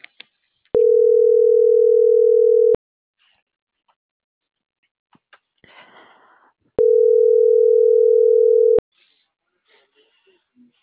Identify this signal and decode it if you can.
Ringback tone (North America), 440 + 480 Hz.
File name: CENTRAL MABE / out mariel.reyes